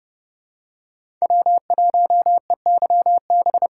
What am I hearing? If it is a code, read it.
Morse code, heard as a keyed tone, about 30 wpm, 700 Hz. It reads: W1EYB
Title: USA Call Signs - 5 Characters - 2x Character Spacing - 30wpm